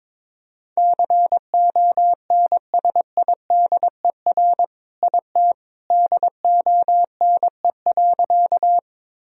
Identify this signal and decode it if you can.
Morse code, heard as a keyed tone, about 22 wpm, 700 Hz. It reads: CONSIDER IT DONE.
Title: Sentences from Top 1000 Words - Review Part 4 - 22wpm